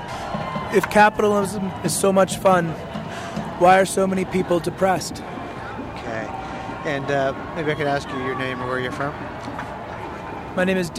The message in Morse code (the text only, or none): none